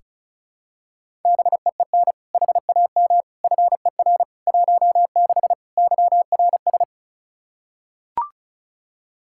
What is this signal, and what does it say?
Morse code, heard as a keyed tone, about 36 wpm, 700 Hz. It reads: BEEN HAM FER 16 YRS
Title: Practice QSO - Exchange 3 - More Info - 35wpm